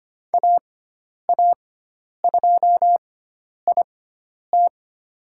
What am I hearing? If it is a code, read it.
Morse code, heard as a keyed tone, about 25 wpm, spaced as for 10 wpm, 700 Hz. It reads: AA2IT
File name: Lesson 16g - Letter c - Call Signs 25wpm